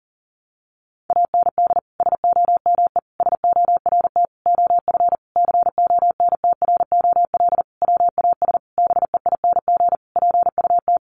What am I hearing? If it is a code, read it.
Morse code, heard as a keyed tone, about 40 wpm, 700 Hz. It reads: AND SOME SORT OF CONTROL WAS BEING PUT
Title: Sentences from Top 1300 Words 40wpm